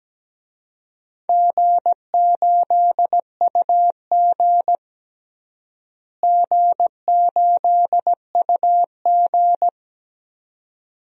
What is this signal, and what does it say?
Morse code, heard as a keyed tone, about 17 wpm, 700 Hz. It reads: G8UG G8UG
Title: DXCC-100 Call Signs - 4 Characters - Repeated 2x - 17wpm